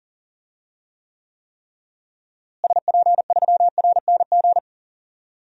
Morse code, heard as a keyed tone, about 40 wpm, 700 Hz. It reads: SP3RNG
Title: DXCC-100 Call Signs - 6 Characters - Repeated 3x - 40wpm